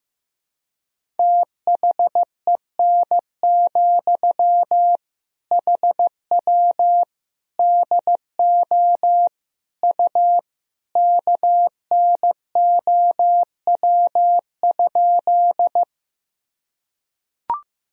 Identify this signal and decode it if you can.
Morse code, heard as a keyed tone, about 15 wpm, 700 Hz. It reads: THEN, HW DO U KNOW?
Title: Sentences from Top 100 Words and Abbreviations 15wpm